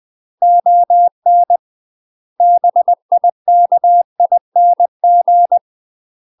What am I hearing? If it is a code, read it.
Morse code, heard as a keyed tone, about 20 wpm, 700 Hz. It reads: ON BIKING
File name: UK MASTER QSOs TRAINING/20WPM